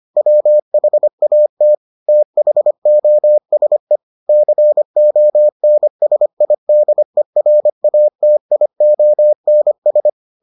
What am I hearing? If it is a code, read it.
Morse code, heard as a keyed tone, about 25 wpm, 600 Hz. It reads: WHAT THOSE CONSIDERATIONS